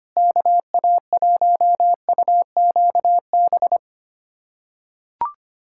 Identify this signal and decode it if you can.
Morse code, heard as a keyed tone, about 25 wpm, 700 Hz. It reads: KA1UQB
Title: USA Callsign DE Callsign - 6 Characters - 25wpm